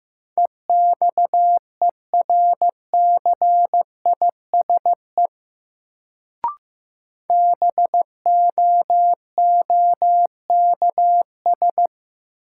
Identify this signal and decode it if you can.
Morse code, heard as a keyed tone, about 15 wpm, 700 Hz. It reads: EXERCISE BOOKS
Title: Sentences from Top 1000 Words - Review Part 4 - New Word Drill - 15wpm